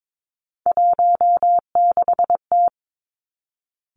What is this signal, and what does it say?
Morse code, heard as a keyed tone, about 22 wpm, 700 Hz. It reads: 16T
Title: Three Challenging Letter-Number Combinations 22wpm